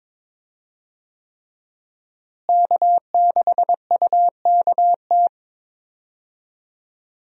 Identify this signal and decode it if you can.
Morse code, heard as a keyed tone, about 22 wpm, 700 Hz. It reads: K6UKT
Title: USA Call Signs - 5 Characters - Repeated 3x - 22wpm